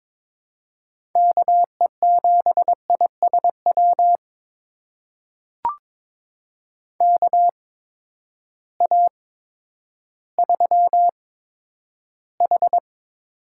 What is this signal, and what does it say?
Morse code, heard as a keyed tone, about 22 wpm, spaced as for 11 wpm, 700 Hz. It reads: KE7ISW KA3H
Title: USA Call Signs - 6 Characters - 8x Character Spacing - 22wpm